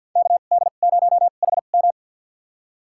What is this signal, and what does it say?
Morse code, heard as a keyed tone, about 48 wpm, spaced as for 35 wpm, 700 Hz. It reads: KD0HM
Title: USA Call Signs - 5 Characters - 2x Character Spacing - 50wpm